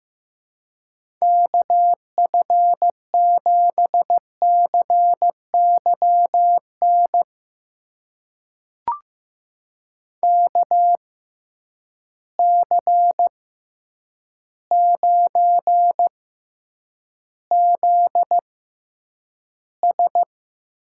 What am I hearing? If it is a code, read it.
Morse code, heard as a keyed tone, about 15 wpm, spaced as for 8 wpm, 700 Hz. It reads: KF7CYN KC9ZS
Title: USA Call Signs - 6 Characters - 6x Character Spacing - 15wpm